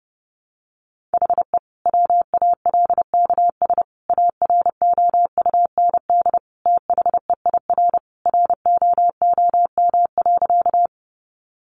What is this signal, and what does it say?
Morse code, heard as a keyed tone, about 30 wpm, 700 Hz. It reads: HE WALKS AROUND THEIR ROOM.
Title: Easy Phrases 30wpm